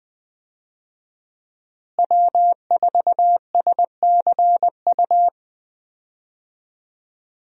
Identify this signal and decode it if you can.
Morse code, heard as a keyed tone, about 20 wpm, 700 Hz. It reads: W4SCU
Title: USA Call Signs - 5 Characters - Repeated 2x - 20wpm